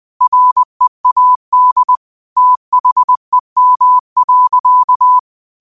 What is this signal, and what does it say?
Morse code, heard as a keyed tone, about 20 wpm, 1000 Hz. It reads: READ THEM.
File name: qotdfiles / 20wpm